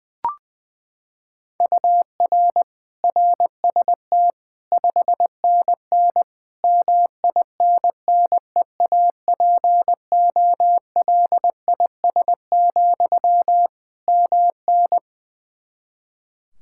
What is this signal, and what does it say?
Morse code, heard as a keyed tone, about 20 wpm, 700 Hz. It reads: UR RST 5NN MINNEAPOLIS, MN
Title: Practice QSO - Exchange 1 - Beginning - 20wpm